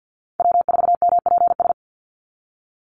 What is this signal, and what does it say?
Morse code, heard as a keyed tone, about 50 wpm, 700 Hz. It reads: W4MPS
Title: NA SOTA Chaser Callsigns 50wpm